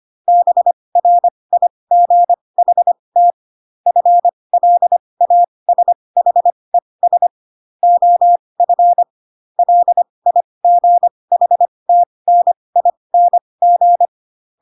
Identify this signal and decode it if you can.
Morse code, heard as a keyed tone, about 25 wpm, spaced as for 20 wpm, 700 Hz. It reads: BRIGHT FLASHES OF LIGHTNING
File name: Pinocchio / Pinocchio ch6 15 25 ik1hge